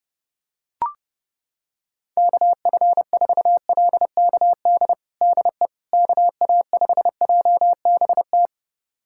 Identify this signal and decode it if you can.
Morse code, heard as a keyed tone, about 30 wpm, 700 Hz. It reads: KF4LKD DE KA5JBT
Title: USA Callsign DE Callsign - 6 Characters - 30wpm